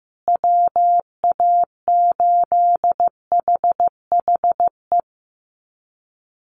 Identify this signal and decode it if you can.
Morse code, heard as a keyed tone, about 15 wpm, 700 Hz. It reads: WA8HHE
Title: USA Call Signs Repeated 2x 15wpm